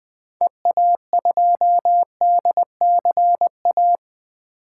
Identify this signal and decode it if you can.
Morse code, heard as a keyed tone, about 20 wpm, 700 Hz. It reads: EA2DCA
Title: 2023 SOTA Activator Call Signs - 20wpm